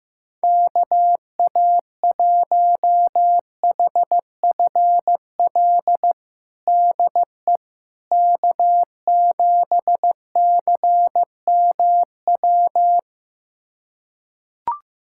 Morse code, heard as a keyed tone, about 15 wpm, 700 Hz. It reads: KA1HFL DE K7CMW
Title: USA Callsign DE Callsign 15wpm